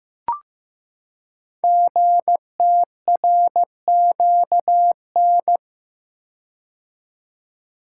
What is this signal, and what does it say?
Morse code, heard as a keyed tone, about 15 wpm, 700 Hz. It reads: GTRQN
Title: Code Groups - 5 Letters - 15wpm